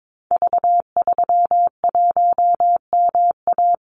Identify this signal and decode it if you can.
Morse code, heard as a keyed tone, about 22 wpm, 700 Hz. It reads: V31MA
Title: Call Signs Repeated 3x - 22wpm